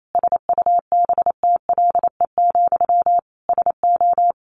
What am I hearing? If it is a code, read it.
Morse code, heard as a keyed tone, about 28 wpm, 700 Hz. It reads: SUBTLE, SO
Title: A Sound Of Thunder 28wpm